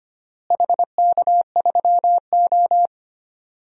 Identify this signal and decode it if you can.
Morse code, heard as a keyed tone, about 25 wpm, 700 Hz. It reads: HK3O
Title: DXCC-100 Call Signs - 4 Characters - Repeated 3x - 25wpm